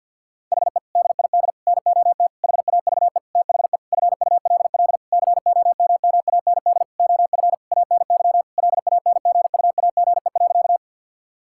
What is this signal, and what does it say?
Morse code, heard as a keyed tone, about 51 wpm, 700 Hz. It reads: HE DID NOT HAVE THE FULL COMMAND OF ANY LANGUAGE.
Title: Sentences from Top 1600 Words 50wpm